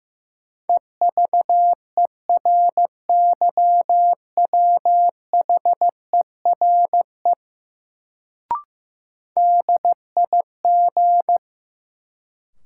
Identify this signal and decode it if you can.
Morse code, heard as a keyed tone, about 15 wpm, 700 Hz. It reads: EVERYWHERE DIG
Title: Sentences from Top 1600 Words - New Word Drill Alt - 15wpm